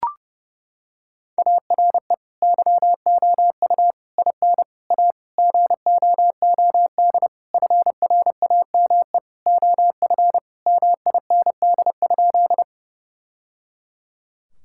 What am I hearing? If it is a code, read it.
Morse code, heard as a keyed tone, about 30 wpm, 700 Hz. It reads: ARE YOU IN A GOOD FRAME OF MIND?
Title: Sentences from Top 1000 Words - Review Part 6 - 30wpm